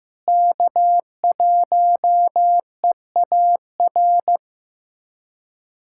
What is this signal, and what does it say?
Morse code, heard as a keyed tone, about 15 wpm, 700 Hz. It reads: K1EAR
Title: USA Callsign DE Callsign 15wpm